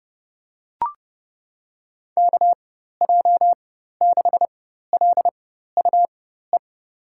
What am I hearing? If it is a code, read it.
Morse code, heard as a keyed tone, about 30 wpm, spaced as for 14 wpm, 700 Hz. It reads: KJ6LUE
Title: USA Call Signs - 6 Characters - 4x Character Spacing - 30wpm